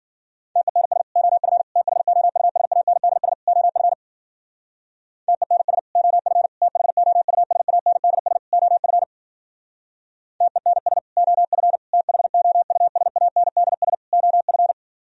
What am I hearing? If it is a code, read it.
Morse code, heard as a keyed tone, about 50 wpm, 700 Hz. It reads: TENS OF THOUSANDS OF TENS OF THOUSANDS OF TENS OF THOUSANDS OF
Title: Speed-Racing - Sets of 4 Words 40wpm